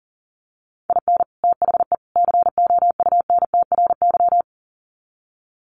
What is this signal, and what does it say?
Morse code, heard as a keyed tone, about 40 wpm, 700 Hz. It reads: IN THE COUNTRY
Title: Sets of 3 Words 40wpm